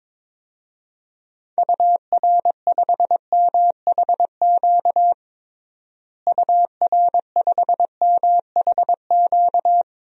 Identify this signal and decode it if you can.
Morse code, heard as a keyed tone, about 22 wpm, 700 Hz. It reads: UR5MHQ UR5MHQ
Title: DXCC-100 Call Signs - 6 Characters - Repeated 3x - 22wpm